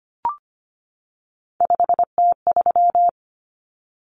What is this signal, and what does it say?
Morse code, heard as a keyed tone, about 25 wpm, 700 Hz. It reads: 5T3
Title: Common Cut Numbers - 3-Digit 25wpm